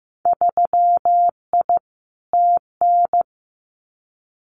Morse code, heard as a keyed tone, about 15 wpm, 700 Hz. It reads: 3I TN
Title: Winter Field Day Exchanges 15wpm